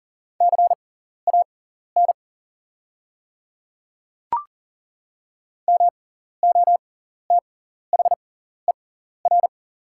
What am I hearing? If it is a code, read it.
Morse code, heard as a keyed tone, about 40 wpm, spaced as for 14 wpm, 700 Hz. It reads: CAN MOTHER
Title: 6x Character Spacing - Top 200 Words - 40wpm